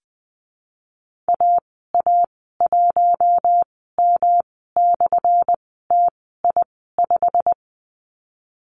Morse code, heard as a keyed tone, about 20 wpm, spaced as for 15 wpm, 700 Hz. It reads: AA1M/TI5
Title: 200 Lesson 32g - Letter k - Call Signs 20wpm